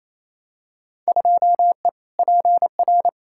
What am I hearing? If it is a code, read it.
Morse code, heard as a keyed tone, about 28 wpm, 700 Hz. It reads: 2E PR
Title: Field Day Exchanges Encore - 28wpm